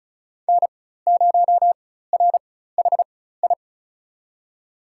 Morse code, heard as a keyed tone, about 34 wpm, spaced as for 16 wpm, 700 Hz. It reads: N0RHI
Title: USA Call Signs - 5 Characters - 4x Character Spacing - 35wpm